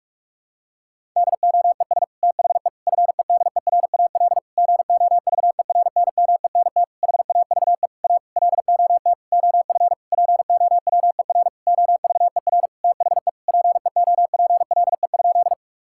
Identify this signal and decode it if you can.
Morse code, heard as a keyed tone, about 46 wpm, 700 Hz. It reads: DOES THE FEDERAL GOVERNMENT HAVE A LOT OF POWER OVER THE PEOPLE?
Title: Sentences from Top 1000 Words - Review Part 5 - 45wpm